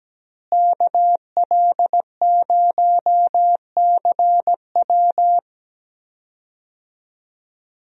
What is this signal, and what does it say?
Morse code, heard as a keyed tone, about 17 wpm, 700 Hz. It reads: KL0CW
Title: USA Call Signs - 5 Characters - Repeated 3x - 17wpm